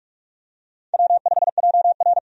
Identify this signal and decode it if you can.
Morse code, heard as a keyed tone, about 46 wpm, 700 Hz. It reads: W5JR
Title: USA Call Signs - 4 Characters - 45wpm